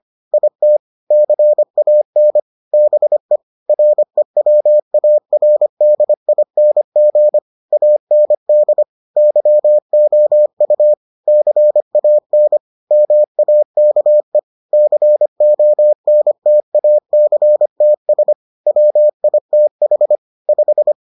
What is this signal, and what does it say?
Morse code, heard as a keyed tone, about 25 wpm, 600 Hz. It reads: IT CAN BE REWARDING AND YOU CAN MAKE CONTACTS WITH 5